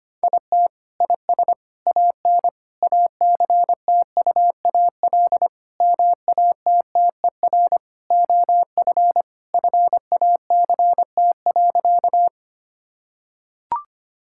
Morse code, heard as a keyed tone, about 25 wpm, 700 Hz. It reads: IT IS AN ACTUAL MATTER OF FACT.
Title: Sentences from Top 1000 Words - Review Part 5 - 25wpm